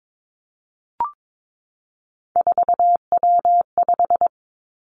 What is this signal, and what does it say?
Morse code, heard as a keyed tone, about 22 wpm, 700 Hz. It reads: 4W5
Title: Three Challenging Letter-Number Combinations 22wpm